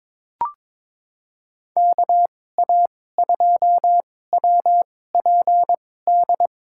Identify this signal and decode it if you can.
Morse code, heard as a keyed tone, about 21 wpm, spaced as for 16 wpm, 700 Hz. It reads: KA2WPD
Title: USA Call Signs - 6 Characters - 2x Character Spacing - 22wpm